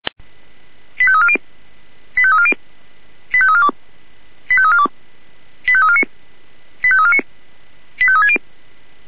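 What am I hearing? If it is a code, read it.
Five-tone selective call (ZVEI), heard as quick runs of five tones, tones of about 70 ms. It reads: 85340 85340 85342 85342 85349 85349 85360
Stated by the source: ZVEI five-tone selective call 85342 received by SDR.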